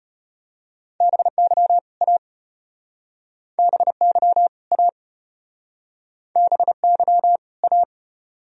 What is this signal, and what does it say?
Morse code, heard as a keyed tone, about 35 wpm, 700 Hz. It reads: BY A BY A BY A